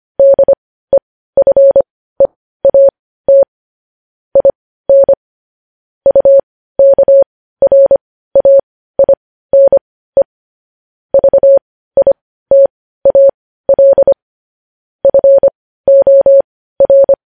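Morse code, heard as a keyed tone, about 25 wpm, spaced as for 15 wpm, 550 Hz. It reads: DEFEAT IN UKRAINE VITAL FOR
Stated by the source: News headlines sent in Morse code at 15 wpm.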